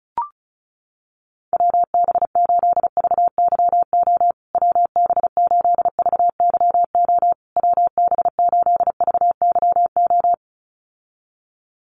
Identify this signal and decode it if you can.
Morse code, heard as a keyed tone, about 34 wpm, 700 Hz. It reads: WB8VYO WB8VYO WB8VYO
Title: USA Call Signs Repeated 3x 35wpm